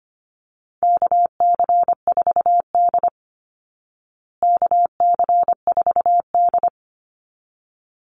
Morse code, heard as a keyed tone, about 25 wpm, 700 Hz. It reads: KC4D KC4D
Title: Call Signs Repeated 3x - 25wpm